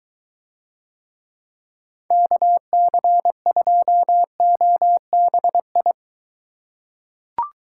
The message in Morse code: KC2OBI

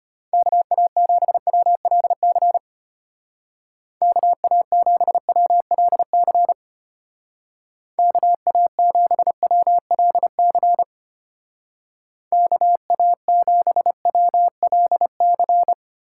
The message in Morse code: KA7WLC KA7WLC KA7WLC KA7WLC

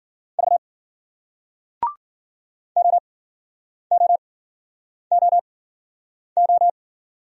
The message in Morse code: V OOOO

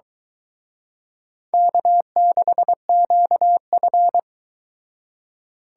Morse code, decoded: K6QF

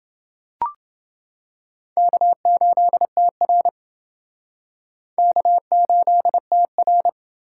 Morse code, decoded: K8TR K8TR

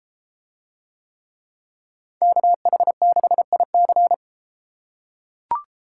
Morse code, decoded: KH6IC

